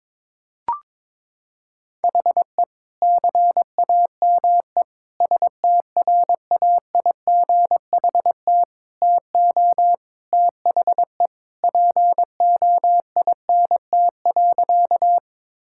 Morse code, heard as a keyed tone, about 22 wpm, 700 Hz. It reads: HE CAME STRAIGHT TO THE POINT.
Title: Sentences from Top 1000 Words - Review Part 6 - 22wpm